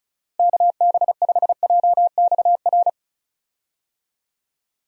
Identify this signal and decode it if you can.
Morse code, heard as a keyed tone, about 36 wpm, 700 Hz. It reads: KB5JXR
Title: USA Call Signs - 6 Characters - Repeated 3x - 35wpm